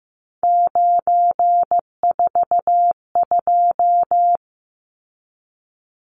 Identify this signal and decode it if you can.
Morse code, heard as a keyed tone, about 15 wpm, 700 Hz. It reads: 942
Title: Numbers - 3-Digit 15wpm